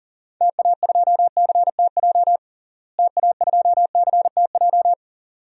Morse code, heard as a keyed tone, about 40 wpm, 700 Hz. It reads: TA2CTJ TA2CTJ
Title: DXCC-100 Call Signs - 6 Characters - Repeated 2x - 40wpm